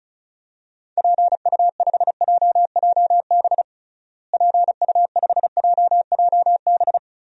Morse code, heard as a keyed tone, about 36 wpm, 700 Hz. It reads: PU5JJB PU5JJB
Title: DXCC-100 Call Signs - 6 Characters - Repeated 3x - 35wpm